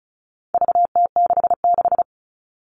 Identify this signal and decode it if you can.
Morse code, heard as a keyed tone, about 34 wpm, 700 Hz. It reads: VT66
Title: All Cut Numbers - 4-Digit 35wpm